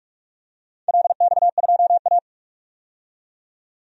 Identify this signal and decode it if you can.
Morse code, heard as a keyed tone, about 44 wpm, 700 Hz. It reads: LX2A
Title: DXCC-100 Call Signs - 4 Characters - 45wpm